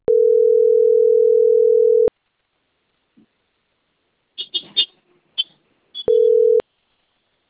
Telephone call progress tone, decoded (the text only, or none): ringback tone